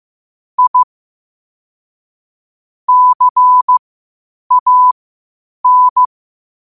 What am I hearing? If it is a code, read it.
Morse code, heard as a keyed tone, about 15 wpm, spaced as for 8 wpm, 1000 Hz. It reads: I CAN
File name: qotdfiles / 8wpm